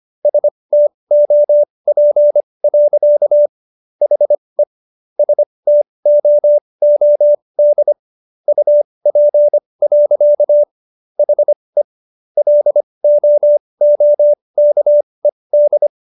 Morse code, heard as a keyed tone, about 25 wpm, spaced as for 20 wpm, 590 Hz. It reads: STOP. HE STOOD UP. HE LOOKED